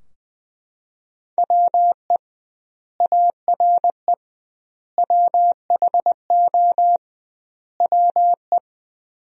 Morse code, heard as a keyed tone, about 20 wpm, 700 Hz. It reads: WE ARE WHO WE
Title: 2x Word-Spacing - Sentences from Top 200 Words - 20wpm